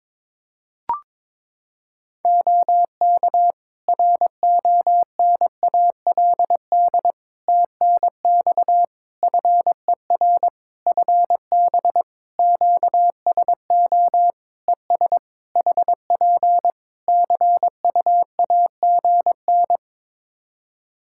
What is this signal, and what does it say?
Morse code, heard as a keyed tone, about 22 wpm, 700 Hz. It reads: OK RONALD TNX FER FB QSO ES HP CUAGN
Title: Practice QSO - Exchange 4 - Ending - 22wpm